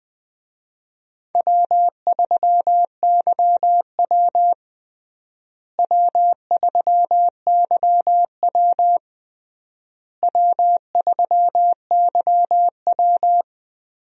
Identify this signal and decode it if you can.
Morse code, heard as a keyed tone, about 20 wpm, 700 Hz. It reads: W3YW W3YW W3YW
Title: USA Call Signs - 4 Characters - Repeated 3x - 20wpm